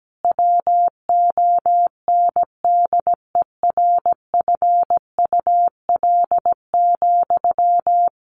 Morse code, heard as a keyed tone, about 17 wpm, 700 Hz. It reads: WONDERFUL,
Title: Sentences from Top 1100 Words 17wpm